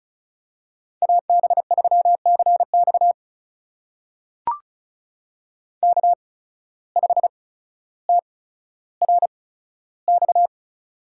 Morse code, heard as a keyed tone, about 36 wpm, spaced as for 15 wpm, 700 Hz. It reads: AB3CX K5TRX
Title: USA Call Signs - 8x Character Spacing - 35wpm